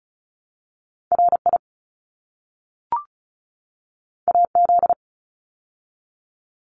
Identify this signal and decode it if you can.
Morse code, heard as a keyed tone, about 34 wpm, 700 Hz. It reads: RI AZ
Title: ARRL Sections 35wpm